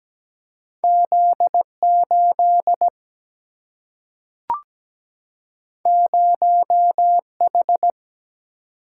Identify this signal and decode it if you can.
Morse code, heard as a keyed tone, about 17 wpm, 700 Hz. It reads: Z8 0H
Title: Single-Letter Single-Number 17wpm